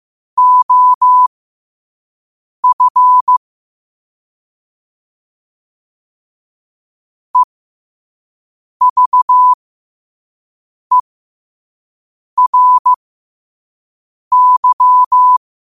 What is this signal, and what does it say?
Morse code, heard as a keyed tone, about 15 wpm, spaced as for 5 wpm, 1000 Hz. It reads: OF EVERY
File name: qotdfiles / 5wpm